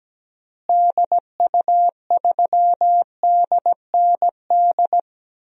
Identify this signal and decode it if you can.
Morse code, heard as a keyed tone, about 17 wpm, 700 Hz. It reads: DU3DND
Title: DXCC-100 Call Signs - 6 Characters - Repeated 2x - 17wpm